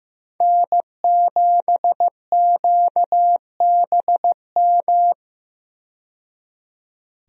Morse code, heard as a keyed tone, about 15 wpm, 700 Hz. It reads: N7QBM
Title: USA Call Signs - 5 Characters - 15wpm